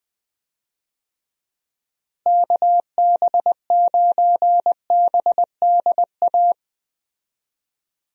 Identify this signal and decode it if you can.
Morse code, heard as a keyed tone, about 20 wpm, 700 Hz. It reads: KB9BDA
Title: USA Call Signs - 6 Characters - Repeated 2x - 20wpm